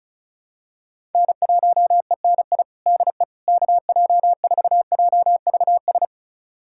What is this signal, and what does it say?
Morse code, heard as a keyed tone, about 36 wpm, 700 Hz. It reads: N1ENI DE KJ4JVS
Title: USA Callsign DE Callsign 35wpm